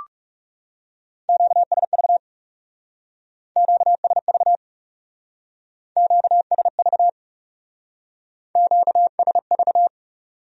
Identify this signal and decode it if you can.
Morse code, heard as a keyed tone, about 36 wpm, 700 Hz. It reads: QSV QSV QSV QSV
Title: Speed-Racing Q-Codes - 30wpm